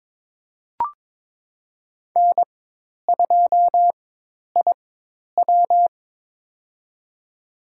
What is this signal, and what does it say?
Morse code, heard as a keyed tone, about 22 wpm, spaced as for 10 wpm, 700 Hz. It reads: N2IW
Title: USA Call Signs - 4 Characters - 4x Character Spacing - 22wpm